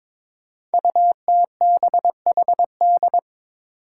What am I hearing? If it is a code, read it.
Morse code, heard as a keyed tone, about 22 wpm, 700 Hz. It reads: UTBHD
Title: Five Challenging Letter-Number Combinations 22wpm